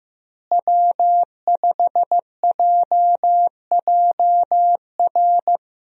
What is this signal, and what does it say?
Morse code, heard as a keyed tone, about 15 wpm, 700 Hz. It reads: W5JJR
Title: USA Call Signs - 5 Characters - Repeated 3x - 15wpm